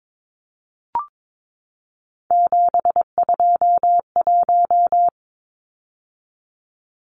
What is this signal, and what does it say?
Morse code, heard as a keyed tone, about 22 wpm, 700 Hz. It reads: 721
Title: Numbers - 3-Digit 22wpm